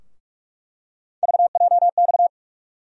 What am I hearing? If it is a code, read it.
Morse code, heard as a keyed tone, about 43 wpm, 700 Hz. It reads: VJX
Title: USA Call Sign Postfixes - 3 Letters 45wpm